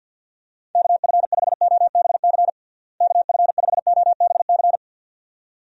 Morse code, heard as a keyed tone, about 47 wpm, 700 Hz. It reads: KF5OBC KF5OBC
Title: USA Call Signs - 6 Characters - Repeated 2x - 50wpm